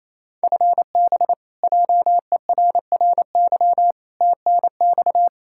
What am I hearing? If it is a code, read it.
Morse code, heard as a keyed tone, about 28 wpm, 700 Hz. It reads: FB JERRY TNX
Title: K1USN SST QSO Practice 28wpm